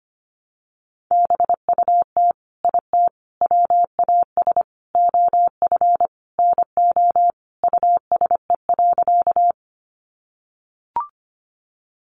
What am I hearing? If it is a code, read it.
Morse code, heard as a keyed tone, about 25 wpm, 700 Hz. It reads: BUT IT WAS OF NO USE.
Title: Sentences from Top 100 Words 25wpm